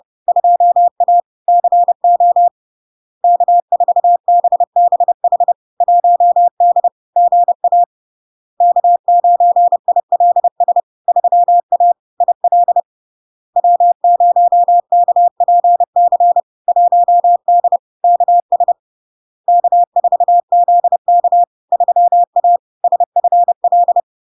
Morse code, heard as a keyed tone, about 30 wpm, 700 Hz. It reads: 2A CO K4BBH 1D GA K9ILS 3A IL W0KPC 1D KS K4ZK 3A SFL
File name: LARK-FD-2022-Log-in-CW-30wpm-700-Hz